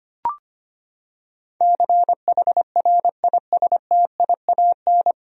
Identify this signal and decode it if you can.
Morse code, heard as a keyed tone, about 25 wpm, 700 Hz. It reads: CHRISTIAN
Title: USA Common Names - 25wpm